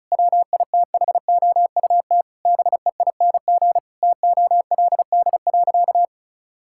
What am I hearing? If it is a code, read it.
Morse code, heard as a keyed tone, about 34 wpm, 700 Hz. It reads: WITHOUT BEING TOLD.
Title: Sentences from Top 1000 Words - Review Part 6 - 35wpm